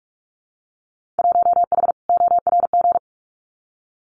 Morse code, heard as a keyed tone, about 46 wpm, 700 Hz. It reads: 1H ORG
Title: Winter Field Day Exchanges 45wpm